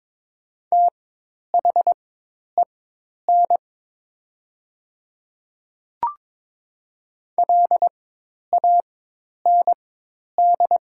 Morse code, heard as a keyed tone, about 22 wpm, spaced as for 10 wpm, 700 Hz. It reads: THEN LAND
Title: 4x Character Spacing - Top 300 Words - 22wpm